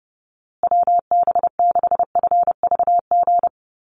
Morse code, heard as a keyed tone, about 30 wpm, 700 Hz. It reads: WB6FVG
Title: USA Call Signs Repeated 3x 30wpm